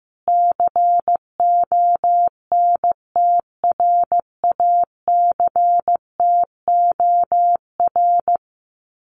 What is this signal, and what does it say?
Morse code, heard as a keyed tone, about 15 wpm, 700 Hz. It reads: CONTRACTOR
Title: Common QSO Phrases Encore 15wpm